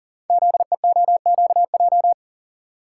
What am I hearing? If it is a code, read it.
Morse code, heard as a keyed tone, about 42 wpm, 700 Hz. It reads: ZEOQJ
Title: Code Groups - 5 Letters - 40wpm